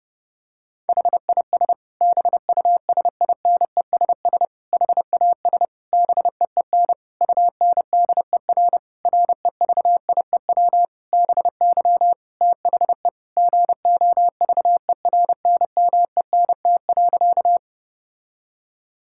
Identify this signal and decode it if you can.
Morse code, heard as a keyed tone, about 29 wpm, 700 Hz. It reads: HIS BUSINESS HAS BEEN UNDER REVIEW BY THE GOVERNMENT.
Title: Sentences from Top 1000 Words - Review Part 6 - 30wpm